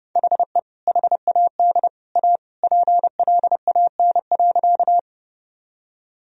Morse code, heard as a keyed tone, about 32 wpm, 700 Hz. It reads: HE HAD A PLAN.
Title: Sentences from Top 1000 Words - Review Part 2 - 30wpm